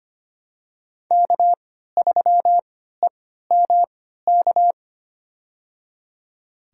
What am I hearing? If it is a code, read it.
Morse code, heard as a keyed tone, about 25 wpm, spaced as for 14 wpm, 700 Hz. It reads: K3EMK
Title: USA Call Signs - 5 Characters - 3x Character Spacing - 25wpm